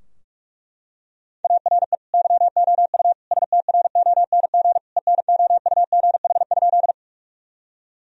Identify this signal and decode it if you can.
Morse code, heard as a keyed tone, about 42 wpm, 700 Hz. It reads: ARE YOU STRONG ENOUGH?
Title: Sentences from Top 1000 Words - Review Part 3 - 45wpm